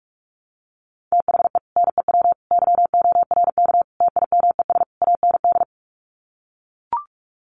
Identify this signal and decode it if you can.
Morse code, heard as a keyed tone, about 44 wpm, 700 Hz. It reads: THE NEW YORK TIMES AND
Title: Sets of 5 Words Encore 45wpm